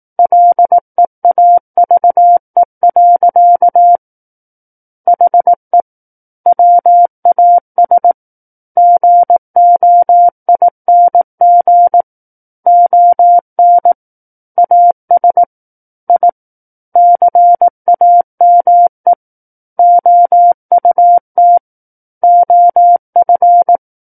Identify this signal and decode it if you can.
Morse code, heard as a keyed tone, about 18 wpm, 700 Hz. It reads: LEAVE. HE WAS GOING ON AS I CAME OUT OF
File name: Book One / 09-Chapter 9 - 18 WPM